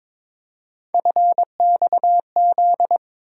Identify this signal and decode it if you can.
Morse code, heard as a keyed tone, about 22 wpm, 700 Hz. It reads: FXZ
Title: USA Call Sign Postfixes - 3 Letters 22wpm